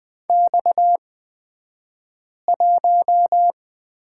Morse code, heard as a keyed tone, about 20 wpm, spaced as for 5 wpm, 700 Hz. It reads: X1
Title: 228 Lesson 37d - Letter x - Two Characters 20wpm